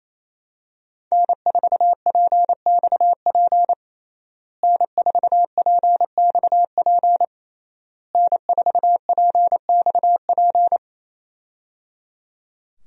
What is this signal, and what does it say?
Morse code, heard as a keyed tone, about 28 wpm, 700 Hz. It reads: N4PXP N4PXP N4PXP
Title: USA Call Signs - 5 Characters - Repeated 3x - 28wpm